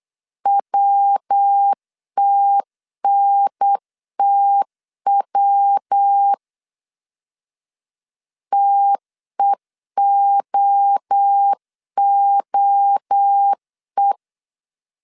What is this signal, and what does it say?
Morse code, heard as a keyed tone, about 8 wpm, 800 Hz. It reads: WTNTW TEOOE